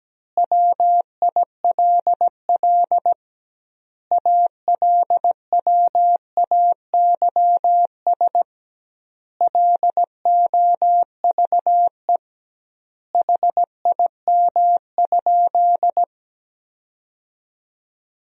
Morse code, heard as a keyed tone, about 17 wpm, 700 Hz. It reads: WILL ALWAYS LOVE HIM?
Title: 2x Word-Spacing - Sentences from Top 400 Words - 17wpm